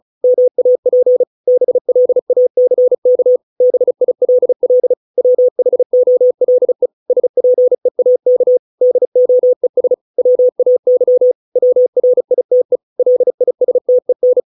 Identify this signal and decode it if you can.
Morse code, heard as a keyed tone, about 34 wpm, 500 Hz. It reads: MAP BLACK BILL WHOLE SPEAK DOES WAY WRITE LISTEN